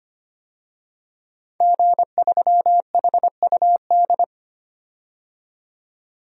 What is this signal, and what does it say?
Morse code, heard as a keyed tone, about 25 wpm, 700 Hz. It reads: G3HUD
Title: DXCC-100 Call Signs - 5 Characters - 25wpm